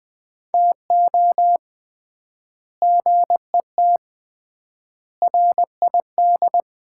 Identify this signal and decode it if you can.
Morse code, heard as a keyed tone, about 20 wpm, 700 Hz. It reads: TO GET RID
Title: 3x Word-Spacing - Sets of 5 Words - 20wpm